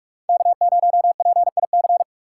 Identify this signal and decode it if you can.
Morse code, heard as a keyed tone, about 43 wpm, 700 Hz. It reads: K0PIC